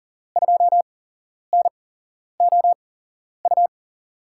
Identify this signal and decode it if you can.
Morse code, heard as a keyed tone, about 40 wpm, spaced as for 11 wpm, 700 Hz. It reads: 2NOU
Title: USA Call Signs - 6 Characters - 8x Character Spacing - 40wpm